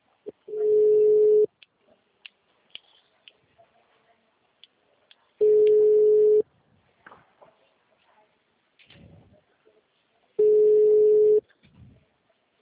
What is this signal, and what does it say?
Ringback tone (Europe, 425 Hz).